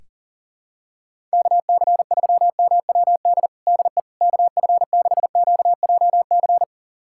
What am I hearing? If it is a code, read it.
Morse code, heard as a keyed tone, about 40 wpm, 700 Hz. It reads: KC3MWD DE KF6QJC
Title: USA Callsign DE Callsign 40wpm